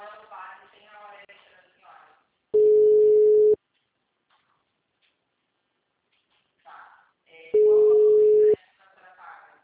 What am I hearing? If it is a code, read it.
Ringback tone (Europe, 425 Hz), with speech.